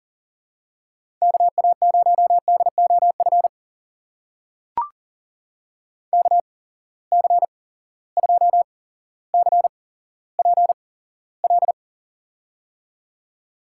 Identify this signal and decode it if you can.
Morse code, heard as a keyed tone, about 42 wpm, spaced as for 17 wpm, 700 Hz. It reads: KA0DOF KC2CPL
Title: USA Call Signs - 6 Characters - 8x Character Spacing - 40wpm